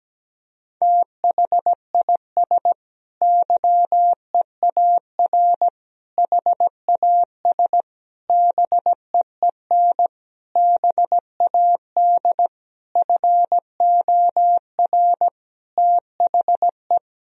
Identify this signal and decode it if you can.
Morse code, heard as a keyed tone, about 17 wpm, 700 Hz. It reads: THIS YEAR HAS BEEN BAD FOR THE
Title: Sentences from Top 1000 Words - Review Part 3 - 17wpm